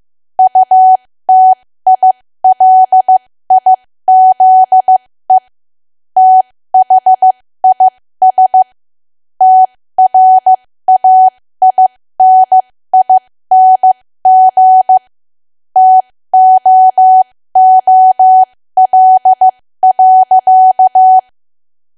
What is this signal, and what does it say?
Morse code, heard as a keyed tone, about 15 wpm, spaced as for 13 wpm, 750 Hz. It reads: UTILIZE THIS TRAINING TOOL.
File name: cwa / W1AW-Level2 / 13WPM